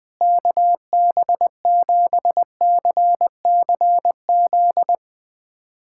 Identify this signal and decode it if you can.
Morse code, heard as a keyed tone, about 21 wpm, 700 Hz. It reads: KB7CCZ
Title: USA Callsign DE Callsign - 6 Characters - 20wpm